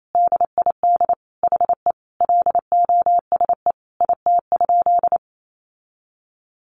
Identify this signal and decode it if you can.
Morse code, heard as a keyed tone, about 28 wpm, 700 Hz. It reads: DID HE LOSE IT?
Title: Sentences from Top 1100 Words 28wpm